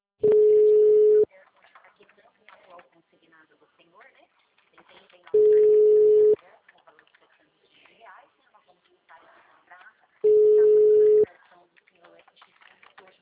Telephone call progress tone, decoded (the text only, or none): ringback tone